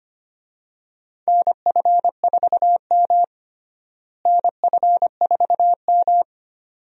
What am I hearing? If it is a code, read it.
Morse code, heard as a keyed tone, about 25 wpm, 700 Hz. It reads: NF4M NF4M
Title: USA Call Signs - 4 Characters - Repeated 2x - 25wpm